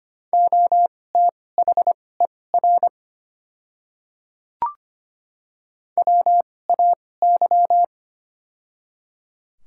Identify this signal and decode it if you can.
Morse code, heard as a keyed tone, about 25 wpm, spaced as for 18 wpm, 700 Hz. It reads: OTHER WAY
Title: 2x Character Spacing - Top 100 Words - 25wpm